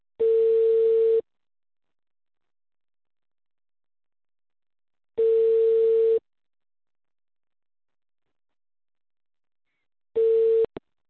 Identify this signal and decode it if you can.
Ringback tone (China, 450 Hz).